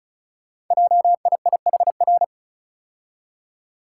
Morse code, heard as a keyed tone, about 36 wpm, 700 Hz. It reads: JIIHR
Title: Code Groups - 5 Letters - 35wpm